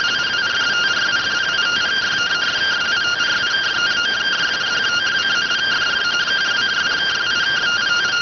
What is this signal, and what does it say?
Two-tone teleprinter signal, about 170 Hz shift.